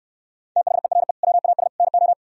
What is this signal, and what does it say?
Morse code, heard as a keyed tone, about 67 wpm, 700 Hz. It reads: THERE WAS NO